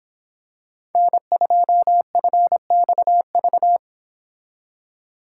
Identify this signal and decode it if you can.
Morse code, heard as a keyed tone, about 26 wpm, 700 Hz. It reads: N2FXV